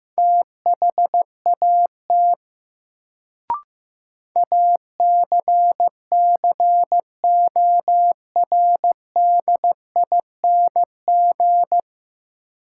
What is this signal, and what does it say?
Morse code, heard as a keyed tone, about 16 wpm, 700 Hz. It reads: THAT ACCORDING
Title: Sets of 2 Words - New Word Drill - 15wpm